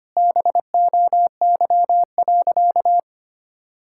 Morse code, heard as a keyed tone, about 25 wpm, 700 Hz. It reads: BOY.
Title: Sentences from Top 1000 Words - Review Part 5 - 25wpm